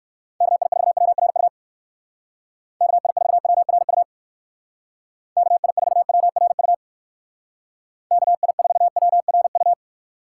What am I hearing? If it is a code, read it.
Morse code, heard as a keyed tone, about 57 wpm, 700 Hz. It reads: KI4WRU KI4WRU KI4WRU KI4WRU